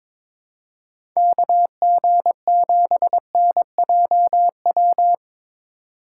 Morse code, heard as a keyed tone, about 21 wpm, 700 Hz. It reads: KG7NJW
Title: USA Call Signs - 6 Characters - 3x Character Spacing - 22wpm